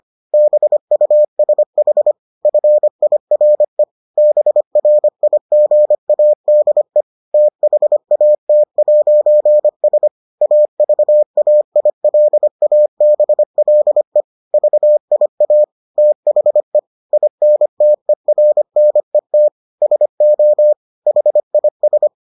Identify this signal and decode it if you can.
Morse code, heard as a keyed tone, about 25 wpm, 600 Hz. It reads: BUSH FIRE BRIGADE THAT'S AVAILABLE VIA THE INTERNET SO HIS